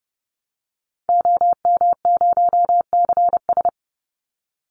Morse code, heard as a keyed tone, about 30 wpm, 700 Hz. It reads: OM0CS
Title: Call Signs Repeated 3x - 30wpm